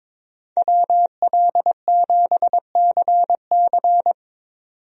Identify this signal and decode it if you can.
Morse code, heard as a keyed tone, about 22 wpm, 700 Hz. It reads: WL7CC
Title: USA Call Signs - 5 Characters - 22wpm